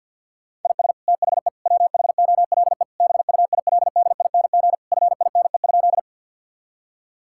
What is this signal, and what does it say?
Morse code, heard as a keyed tone, about 51 wpm, 700 Hz. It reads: IS THE WHOLE BUILDING FINE?
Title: Sentences from Top 1000 Words - Review Part 5 - 50wpm